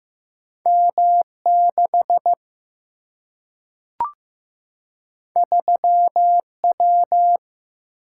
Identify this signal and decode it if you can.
Morse code, heard as a keyed tone, about 15 wpm, 700 Hz. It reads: M 6 3 W
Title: Single-Letter Single-Number 15wpm